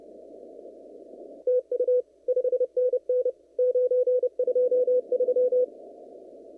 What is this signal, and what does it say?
Morse code, heard as a keyed tone, about 30 wpm, 500 Hz. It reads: TU 5NN 923
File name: CW / kt5j wpx cw-op-k5pi